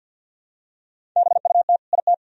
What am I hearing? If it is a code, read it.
Morse code, heard as a keyed tone, about 46 wpm, 700 Hz. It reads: BUT IT
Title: Sentences from Top 100 Words and Abbreviations 50wpm